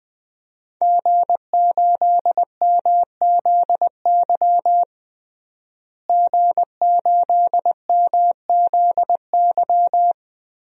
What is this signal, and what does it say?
Morse code, heard as a keyed tone, about 20 wpm, 700 Hz. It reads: G8MZY G8MZY
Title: DXCC-100 Call Signs - 5 Characters - Repeated 2x - 20wpm